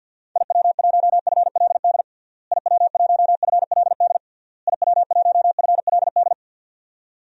Morse code, heard as a keyed tone, about 51 wpm, 700 Hz. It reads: IW1FLD IW1FLD IW1FLD
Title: DXCC-100 Call Signs - 6 Characters - Repeated 3x - 50wpm